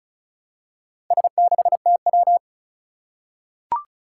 Morse code, heard as a keyed tone, about 36 wpm, 700 Hz. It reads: S6TW
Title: Four Challenging Letter-Number Combinations 35wpm